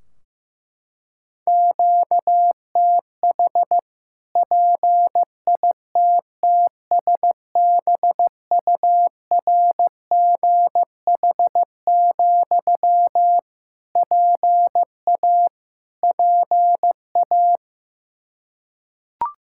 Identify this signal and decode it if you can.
Morse code, heard as a keyed tone, about 15 wpm, 700 Hz. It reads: QTH PITTSBURGH, PA PA
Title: Practice QSO - Exchange 1 - Beginning - 15wpm